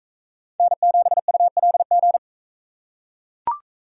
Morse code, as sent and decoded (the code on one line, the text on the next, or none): -. --... ..- .-.. --.
N7ULG